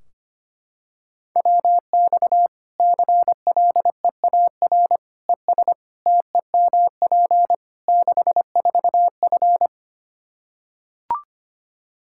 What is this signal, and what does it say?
Morse code, heard as a keyed tone, about 25 wpm, 700 Hz. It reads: WX CLEAR ES TEMP 64F
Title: K1USN SST QSO Practice 25wpm